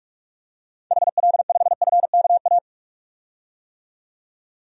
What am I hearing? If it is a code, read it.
Morse code, heard as a keyed tone, about 41 wpm, 700 Hz. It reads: HL5FKA
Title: DXCC-100 Call Signs - 6 Characters - 45wpm